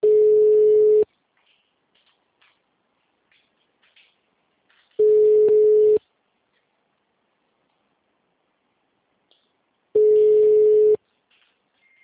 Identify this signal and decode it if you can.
Ringback tone (Europe, 425 Hz).